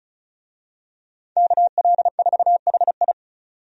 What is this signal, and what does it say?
Morse code, heard as a keyed tone, about 34 wpm, 700 Hz. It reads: KL4HI
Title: USA Call Signs - 5 Characters - 35wpm